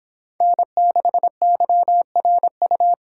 Morse code, heard as a keyed tone, about 30 wpm, 700 Hz. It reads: N6YRU